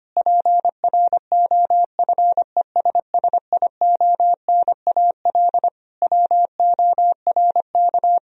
Morse code, heard as a keyed tone, about 25 wpm, 700 Hz. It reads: PROFESSIONAL WORK
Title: Sentences from Top 1000 Words - Review Part 4 - 25wpm